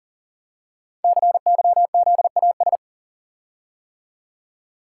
Morse code, heard as a keyed tone, about 42 wpm, 700 Hz. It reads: CYZAS
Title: Code Groups - 5 Letters - 40wpm